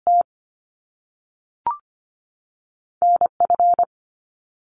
Morse code, heard as a keyed tone, about 25 wpm, 700 Hz. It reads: T N F
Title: Top Two Letter Combinations in Words 25wpm